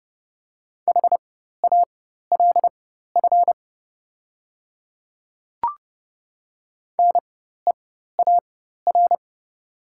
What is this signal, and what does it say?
Morse code, heard as a keyed tone, about 30 wpm, spaced as for 14 wpm, 700 Hz. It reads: HALF NEAR
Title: 4x Character Spacing - Top 500 Words - 30wpm